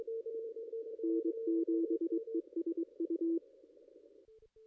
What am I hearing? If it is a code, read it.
Morse code, heard as a keyed tone, about 22 wpm, 330 Hz. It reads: N7ESU
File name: contests / ss-cw-k5tr-op-n5rz -2200